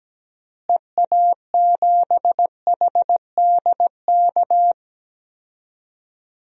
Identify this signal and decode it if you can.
Morse code, heard as a keyed tone, about 17 wpm, 700 Hz. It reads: EA7HDK
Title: DXCC-100 Call Signs - 6 Characters - Repeated 3x - 17wpm